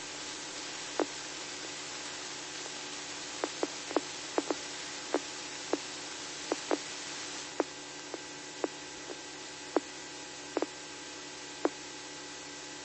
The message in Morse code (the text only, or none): none